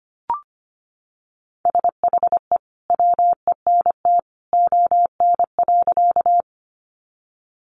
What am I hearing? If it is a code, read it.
Morse code, heard as a keyed tone, about 25 wpm, 700 Hz. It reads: SHE WENT ON.
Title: Sentences from Top 200 Words 25wpm